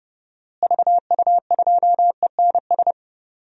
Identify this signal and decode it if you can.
Morse code, heard as a keyed tone, about 30 wpm, 700 Hz. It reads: VU2ENS
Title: DXCC-100 Call Signs - 6 Characters - Repeated 3x - 30wpm